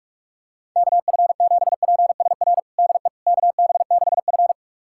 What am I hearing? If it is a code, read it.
Morse code, heard as a keyed tone, about 47 wpm, 690 Hz. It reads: KF7PSR DE KB6F
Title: USA Callsign DE Callsign 45wpm